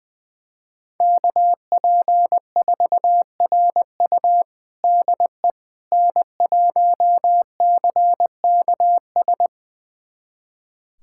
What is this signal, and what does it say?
Morse code, heard as a keyed tone, about 20 wpm, 700 Hz. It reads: KP4RU DE N1CKS
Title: USA Callsign DE Callsign - 5 Characters - 20wpm